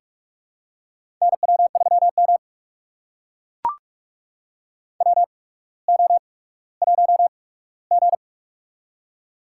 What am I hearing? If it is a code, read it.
Morse code, heard as a keyed tone, about 44 wpm, spaced as for 19 wpm, 700 Hz. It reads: NW3M WO1G
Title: USA Call Signs - 4 Characters - 8x Character Spacing - 45wpm